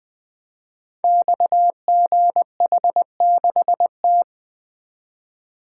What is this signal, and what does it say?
Morse code, heard as a keyed tone, about 20 wpm, 700 Hz. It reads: XGH6T
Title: Code Groups - 5 Letter:Numbers - 20wpm